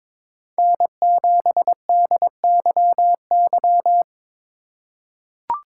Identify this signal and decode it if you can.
Morse code, heard as a keyed tone, about 22 wpm, 700 Hz. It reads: N7DYY
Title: USA Call Signs - 5 Characters - 2x Character Spacing - 22wpm